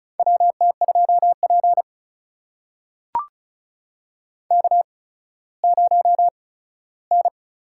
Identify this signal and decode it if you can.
Morse code, heard as a keyed tone, about 36 wpm, spaced as for 17 wpm, 700 Hz. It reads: WT2P K0N
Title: USA Call Signs - 4 Characters - 8x Character Spacing - 35wpm